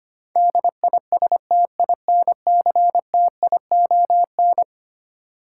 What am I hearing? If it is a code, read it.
Morse code, heard as a keyed tone, about 25 wpm, 700 Hz. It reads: DISTINCTION
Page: morse-code-ninja-practice-25wpm